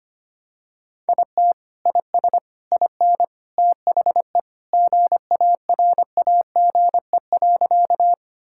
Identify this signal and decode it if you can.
Morse code, heard as a keyed tone, about 25 wpm, 700 Hz. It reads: IT IS IN THE GARAGE.
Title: Sentences from Top 1600 Words 25wpm